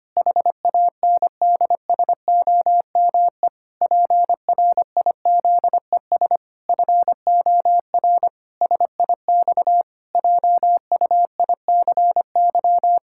Morse code, heard as a keyed tone, about 25 wpm, 700 Hz. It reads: HANDSOME PRIZES FOR SIX JUICY
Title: Pangram Challenge v2 - 25wpm